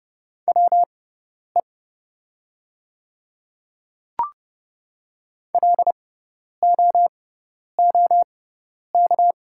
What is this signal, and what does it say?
Morse code, heard as a keyed tone, about 30 wpm, spaced as for 10 wpm, 690 Hz. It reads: WE LOOK
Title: 6x Character Spacing - Top 500 Words - 30wpm